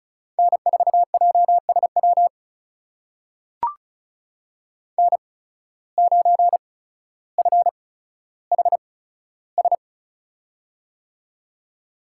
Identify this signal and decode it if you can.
Morse code, heard as a keyed tone, about 36 wpm, spaced as for 15 wpm, 700 Hz. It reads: N4JSW N9FHS
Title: USA Call Signs - 5 Characters - 8x Character Spacing - 35wpm